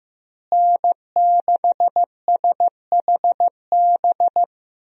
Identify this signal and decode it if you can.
Morse code, heard as a keyed tone, about 15 wpm, 700 Hz. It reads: N6SHB
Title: USA Call Signs - 5 Characters - Repeated 2x - 15wpm